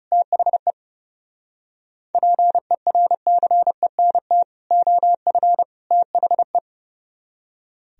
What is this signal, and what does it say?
Morse code, heard as a keyed tone, about 36 wpm, 700 Hz. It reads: THE PERCENT OF THE
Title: Speed-Racing - Sets of 3 Words 30wpm